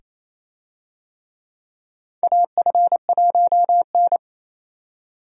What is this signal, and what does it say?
Morse code, heard as a keyed tone, about 28 wpm, 700 Hz. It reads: AF1N
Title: USA Call Signs - 4 Characters - 28wpm